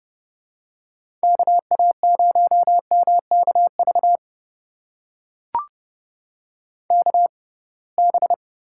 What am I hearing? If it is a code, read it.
Morse code, heard as a keyed tone, about 30 wpm, spaced as for 23 wpm, 700 Hz. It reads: KA0MKV KB
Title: USA Call Signs - 6 Characters - 6x Character Spacing - 30wpm